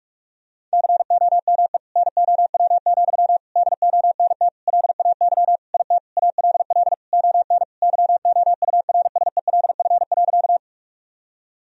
Morse code, heard as a keyed tone, about 46 wpm, 700 Hz. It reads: COME NOW, DONT LAY IT ALL ON YOURSELF.
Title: Sentences from Top 1000 Words - Review Part 2 - 45wpm